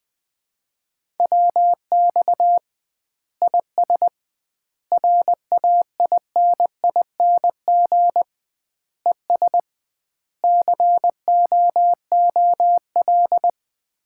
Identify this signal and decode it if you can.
Morse code, heard as a keyed tone, about 20 wpm, 700 Hz. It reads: WX IS RAINING ES COOL
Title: Common QSO Phrases - 2x Word-Spacing - 20wpm